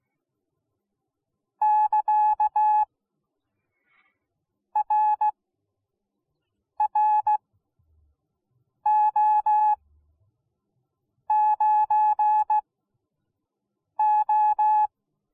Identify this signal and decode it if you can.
Morse code, heard as a keyed tone, about 17 wpm, spaced as for 5 wpm, 850 Hz. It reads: <KA> RRO9O